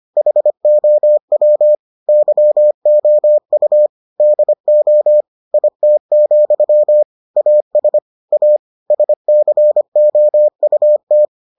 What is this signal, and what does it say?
Morse code, heard as a keyed tone, about 25 wpm, 600 Hz. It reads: HOW YOU DO IT, AS A SCOUT